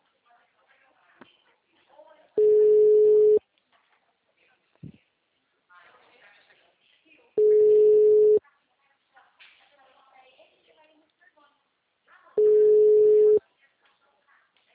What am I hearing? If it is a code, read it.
Ringback tone (Europe, 425 Hz), with speech.